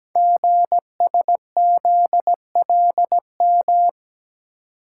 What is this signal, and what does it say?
Morse code, heard as a keyed tone, about 17 wpm, 700 Hz. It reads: GSZLM
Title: Code Groups - 5 Letters - 17wpm